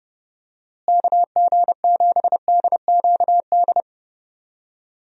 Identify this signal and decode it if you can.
Morse code, heard as a keyed tone, about 30 wpm, 700 Hz. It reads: KG7DQD
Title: USA Call Signs - 6 Characters - Repeated 2x - 30wpm